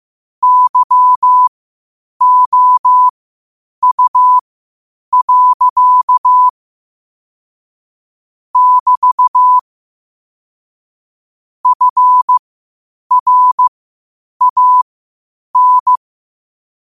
Morse code, heard as a keyed tone, about 15 wpm, spaced as for 8 wpm, 1000 Hz. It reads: YOU. <BT> FRAN
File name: qotdfiles / 8wpm